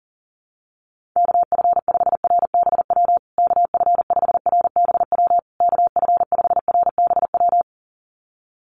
Morse code, heard as a keyed tone, about 40 wpm, 700 Hz. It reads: KF5RBW KF5RBW KF5RBW
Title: USA Call Signs Repeated 3x 40wpm